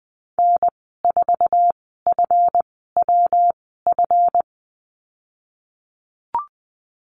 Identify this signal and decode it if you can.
Morse code, heard as a keyed tone, about 19 wpm, spaced as for 14 wpm, 700 Hz. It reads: N4FWF
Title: 161 Lesson 26g - Letter g - Call Signs 20wpm